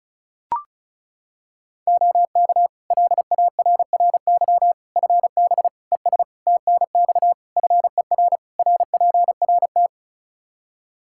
Morse code, heard as a keyed tone, about 36 wpm, 700 Hz. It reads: OK LARRY FB ES TNX FER RPRT
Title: Practice QSO - Exchange 2 - Info - 35wpm